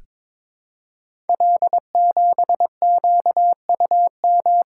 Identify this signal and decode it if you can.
Morse code, heard as a keyed tone, about 22 wpm, 700 Hz. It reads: L7QUM
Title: Code Groups - 5 Letter:Numbers - 22wpm